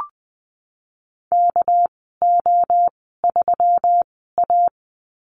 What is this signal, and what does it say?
Morse code, heard as a keyed tone, about 20 wpm, spaced as for 15 wpm, 700 Hz. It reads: KO3A
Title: 200 Lesson 32g - Letter k - Call Signs 20wpm